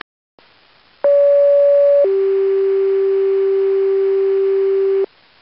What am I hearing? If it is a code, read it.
Two-tone page, heard as a one-second tone, then a three-second tone, about 570 then 390 Hz, with noise. It is heard from 1.0 to 5.1 s.